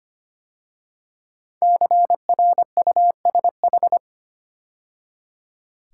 Morse code, heard as a keyed tone, about 25 wpm, 700 Hz. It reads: CRUSH
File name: 5 Letter Words - First 3 Letters Given - 25wpm